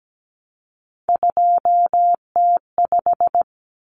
Morse code, heard as a keyed tone, about 17 wpm, 700 Hz. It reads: 2T5
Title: Common Cut Numbers - 3-Digit 17wpm